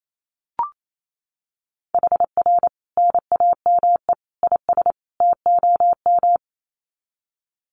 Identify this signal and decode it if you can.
Morse code, heard as a keyed tone, about 28 wpm, 700 Hz. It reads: HR NAME IS TOM
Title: Common QSO Phrases 28wpm